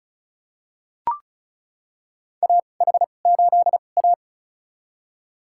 Morse code, heard as a keyed tone, about 34 wpm, spaced as for 25 wpm, 700 Hz. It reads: AH8A
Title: USA Call Signs - 4 Characters - 2x Character Spacing - 35wpm